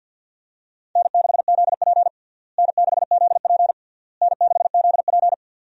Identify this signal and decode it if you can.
Morse code, heard as a keyed tone, about 52 wpm, 700 Hz. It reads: N6ZP N6ZP N6ZP
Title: USA Call Signs - 4 Characters - Repeated 3x - 50wpm